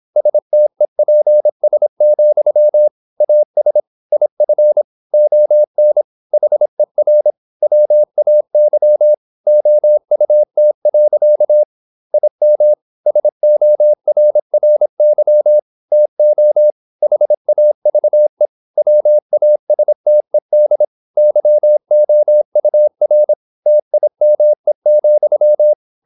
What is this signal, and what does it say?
Morse code, heard as a keyed tone, about 26 wpm, 600 Hz. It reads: STEPS, AS IF ON HER WAY OUT. IM SORRY TO HAVE WASTED YOUR TIME,
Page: Never 26wpm | CW med Gnister